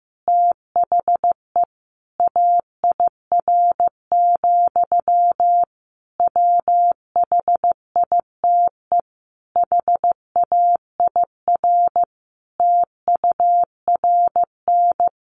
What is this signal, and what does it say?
Morse code, heard as a keyed tone, about 15 wpm, 700 Hz. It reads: THE AIR, WHITE HAIR TURN
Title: A Sound Of Thunder 15wpm